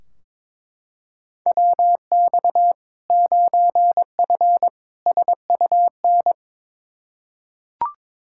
Morse code, heard as a keyed tone, about 22 wpm, 700 Hz. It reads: WX 9F SUN
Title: Practice QSO - Exchange 2 - Info - 22wpm